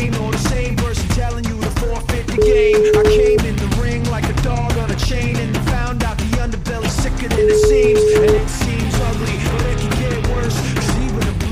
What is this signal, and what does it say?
Ringback tone (Europe, 425 Hz), with noise.